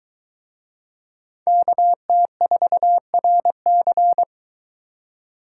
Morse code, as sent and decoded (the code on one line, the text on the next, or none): -.- - ....- .-. -.-.
KT4RC